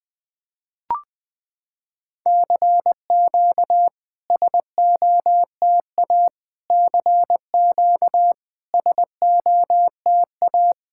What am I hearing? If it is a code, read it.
Morse code, heard as a keyed tone, about 20 wpm, 690 Hz. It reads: CQ SOTA CQ SOTA
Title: SOTA QSO Practice 20wpm